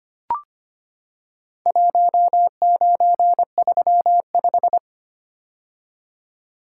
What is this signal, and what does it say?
Morse code, heard as a keyed tone, about 25 wpm, 700 Hz. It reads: 1935
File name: practice / 25wpm / Four Letter-Number Combinations - 25wpm